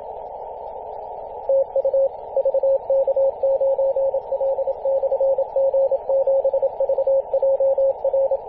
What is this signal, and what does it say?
Morse code, heard as a keyed tone, about 27 wpm, 540 Hz, with radio noise. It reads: TU VK9L/G7VJR
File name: posts / 2016-arp-15-2115z-vk9l-g7vjr-18-081mhz-cw